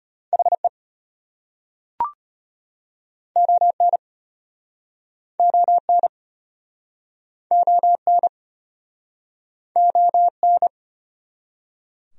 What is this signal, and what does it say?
Morse code, heard as a keyed tone, about 36 wpm, 700 Hz. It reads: HE ON ON ON ON